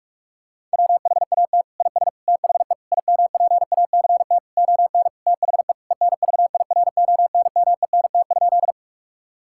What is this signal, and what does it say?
Morse code, heard as a keyed tone, about 46 wpm, 700 Hz. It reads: WHAT IS THE IMPACT ON THE ENVIRONMENT?
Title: Sentences from Top 1000 Words - Review Part 6 - 45wpm